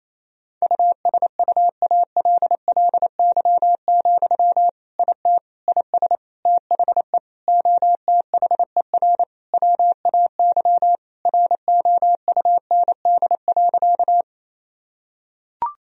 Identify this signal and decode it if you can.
Morse code, heard as a keyed tone, about 28 wpm, 700 Hz. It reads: USUALLY, IT IS THE OTHER WAY ROUND.
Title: Sentences from Top 1000 Words - Review Part 3 - 28wpm